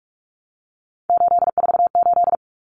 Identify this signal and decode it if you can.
Morse code, heard as a keyed tone, about 44 wpm, 700 Hz. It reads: 848
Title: Common Cut Numbers - 3-Digit 45wpm